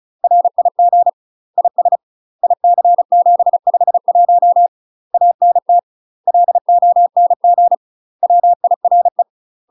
Morse code, heard as a keyed tone, about 34 wpm, 700 Hz. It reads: RIG IS IC751 ANT LONG WIRE
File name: UK MASTER QSOs TRAINING / 35WPM / UK MASTER QSOs 0040